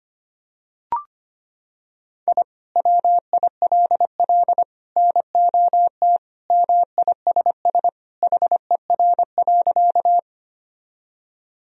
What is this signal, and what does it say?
Morse code, heard as a keyed tone, about 25 wpm, 700 Hz. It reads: I WILL NOT MISS HER.
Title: Sentences from Top 1000 Words - Review Part 3 - 25wpm